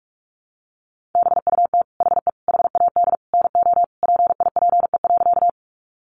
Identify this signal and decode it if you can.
Morse code, heard as a keyed tone, about 44 wpm, 700 Hz. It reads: BUT HE HAD NO PIPE.
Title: Sentences from Top 1500 Words 45wpm